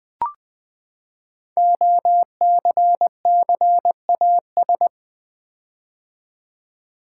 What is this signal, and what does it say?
Morse code, heard as a keyed tone, about 20 wpm, 700 Hz. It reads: OCCAS
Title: Build A Word - 5 Letters At A Time - 12-Letter Words - 20wpm